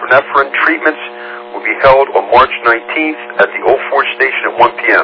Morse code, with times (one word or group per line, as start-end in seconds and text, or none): none